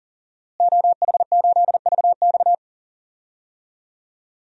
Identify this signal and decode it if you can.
Morse code, heard as a keyed tone, about 40 wpm, 700 Hz. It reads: OH8VX
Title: DXCC-100 Call Signs - 5 Characters - Repeated 2x - 40wpm